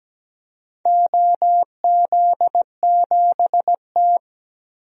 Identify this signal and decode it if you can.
Morse code, heard as a keyed tone, about 17 wpm, 700 Hz. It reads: OZ7T
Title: DXCC-100 Call Signs - 4 Characters - Repeated 3x - 17wpm